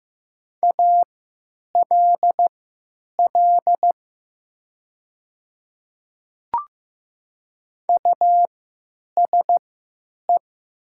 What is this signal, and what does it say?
Morse code, heard as a keyed tone, about 15 wpm, spaced as for 9 wpm, 700 Hz. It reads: ALL USE
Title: 3x Character Spacing - Top 100 Words - 15wpm